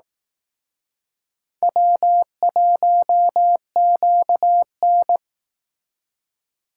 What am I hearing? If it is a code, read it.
Morse code, heard as a keyed tone, about 19 wpm, 700 Hz. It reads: W1QN